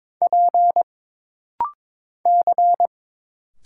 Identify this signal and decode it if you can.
Morse code, heard as a keyed tone, about 22 wpm, 700 Hz. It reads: P C
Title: Single Letter-Number Alt - Rapid-Fire 22wpm